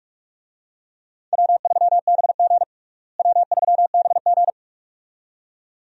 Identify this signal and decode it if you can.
Morse code, heard as a keyed tone, about 41 wpm, 700 Hz. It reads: W3BG W3BG
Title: USA Call Signs - 4 Characters - Repeated 2x - 45wpm